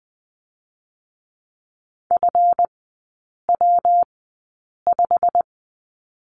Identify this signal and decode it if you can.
Morse code, heard as a keyed tone, about 20 wpm, spaced as for 8 wpm, 690 Hz. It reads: FW5
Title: 159 Lesson 26e - Letter g - Three Characters 20wpm